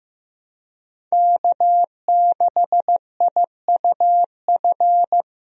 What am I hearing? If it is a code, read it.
Morse code, heard as a keyed tone, about 16 wpm, 700 Hz. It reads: K6IUF
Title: USA Call Signs - 5 Characters - Repeated 2x - 15wpm